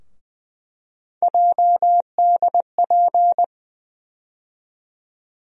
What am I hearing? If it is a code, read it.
Morse code, heard as a keyed tone, about 20 wpm, 700 Hz. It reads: JDP
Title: USA Call Sign Postfixes - 3 Letters 20wpm